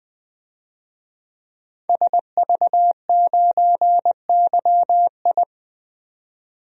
Morse code, heard as a keyed tone, about 20 wpm, 700 Hz. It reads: SV9YI
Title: DXCC-100 Call Signs - 5 Characters - 20wpm